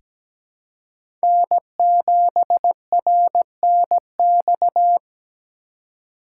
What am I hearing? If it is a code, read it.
Morse code, heard as a keyed tone, about 17 wpm, 700 Hz. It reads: N7RNX
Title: USA Call Signs - 5 Characters - 4x Character Spacing - 17wpm